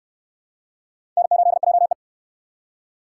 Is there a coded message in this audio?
Yes: Morse code from 1.2 to 1.9 s, heard as a keyed tone, about 59 wpm, 700 Hz. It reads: N8JE